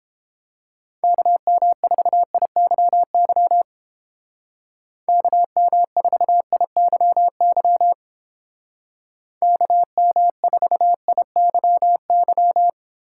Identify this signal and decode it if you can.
Morse code, heard as a keyed tone, about 30 wpm, 700 Hz. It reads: KM4IYY KM4IYY KM4IYY